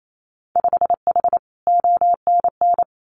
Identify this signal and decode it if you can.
Morse code, heard as a keyed tone, about 28 wpm, 700 Hz. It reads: 5H ONN
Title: Winter Field Day Exchanges Encore 28wpm